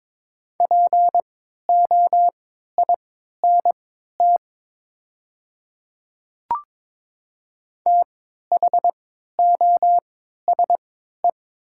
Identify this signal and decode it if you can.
Morse code, heard as a keyed tone, about 22 wpm, spaced as for 12 wpm, 700 Hz. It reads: POINT THOSE
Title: 3x Character Spacing - Top 500 Words - 22wpm